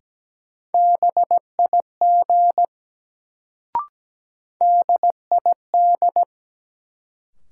Morse code, heard as a keyed tone, about 17 wpm, 700 Hz. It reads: BIG DID
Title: Sets of 2 Words - New Word Drill - 17wpm